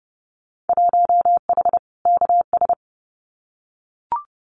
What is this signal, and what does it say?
Morse code, heard as a keyed tone, about 30 wpm, 700 Hz. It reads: 1H KS
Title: Winter Field Day Exchanges 30wpm